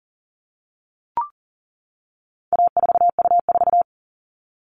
Morse code, heard as a keyed tone, about 40 wpm, 700 Hz. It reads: A4U4
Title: All Cut Numbers - 4-Digit 40wpm